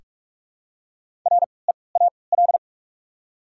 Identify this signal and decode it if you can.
Morse code, heard as a keyed tone, about 46 wpm, spaced as for 25 wpm, 690 Hz. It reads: REAL
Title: 3x Character Spacing - Top 300 Words - 45wpm